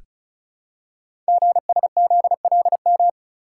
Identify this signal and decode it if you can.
Morse code, heard as a keyed tone, about 36 wpm, 700 Hz. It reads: GSZLM
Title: Code Groups - 5 Letters - 35wpm